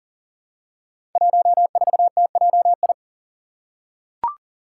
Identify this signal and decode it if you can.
Morse code, heard as a keyed tone, about 38 wpm, 700 Hz. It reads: 14TJI
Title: Five Challenging Letter-Number Combinations 40wpm